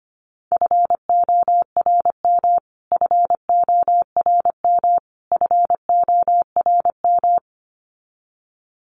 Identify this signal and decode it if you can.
Morse code, heard as a keyed tone, about 25 wpm, 700 Hz. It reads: FORM FORM FORM
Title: Top 500 Words Repeated 25wpm